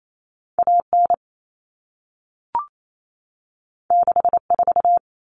Morse code, heard as a keyed tone, about 28 wpm, 700 Hz. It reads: AN 64
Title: Common Cut Numbers - 2-Digit 28wpm